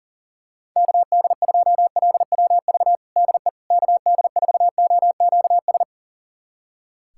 Morse code, heard as a keyed tone, about 37 wpm, 700 Hz. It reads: KD2LWV DE KD4OQS
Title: USA Callsign DE Callsign - 6 Characters - 40wpm